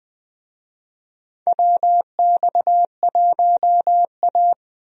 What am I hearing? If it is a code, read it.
Morse code, heard as a keyed tone, about 21 wpm, 700 Hz. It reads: WX1A
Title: USA Call Signs - 4 Characters - Repeated 2x - 20wpm